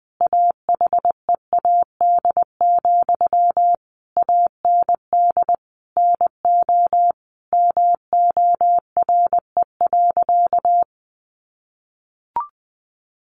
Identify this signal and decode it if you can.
Morse code, heard as a keyed tone, about 20 wpm, 700 Hz. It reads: AHEAD, AND NO MORE.
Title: Sentences from Top 1100 Words 20wpm